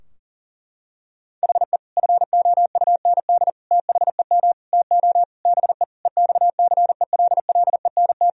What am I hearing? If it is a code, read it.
Morse code, heard as a keyed tone, about 40 wpm, 700 Hz. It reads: HE FOUND THEM TO BE EXCELLENT
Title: Sentences from Top 1000 Words - Review Part 3 - 40wpm